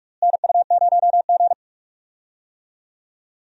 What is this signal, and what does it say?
Morse code, heard as a keyed tone, about 46 wpm, 700 Hz. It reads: NU0G
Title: USA Call Signs - 4 Characters - Repeated 2x - 45wpm